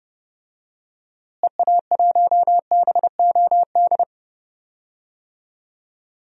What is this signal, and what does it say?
Morse code, heard as a keyed tone, about 29 wpm, 700 Hz. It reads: EA1BOD
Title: DXCC-100 Call Signs - 6 Characters - Repeated 2x - 30wpm